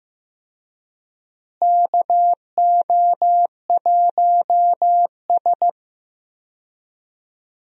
Morse code, heard as a keyed tone, about 15 wpm, 700 Hz. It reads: KO1S
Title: USA Call Signs - 4 Characters - Repeated 3x - 15wpm